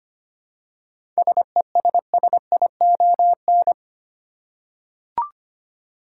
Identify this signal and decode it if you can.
Morse code, heard as a keyed tone, about 25 wpm, 700 Hz. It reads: SESSION
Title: Seven Letter Words - 25wpm